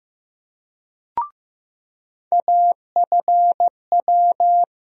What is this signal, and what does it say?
Morse code, heard as a keyed tone, about 15 wpm, 700 Hz. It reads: AFW
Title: USA Call Sign Postfixes - 3 Letters 15wpm